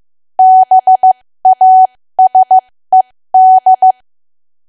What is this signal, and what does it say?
Morse code, heard as a keyed tone, about 15 wpm, spaced as for 13 wpm, 750 Hz. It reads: BASED
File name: cwa / W1AW-Level2 / 13WPM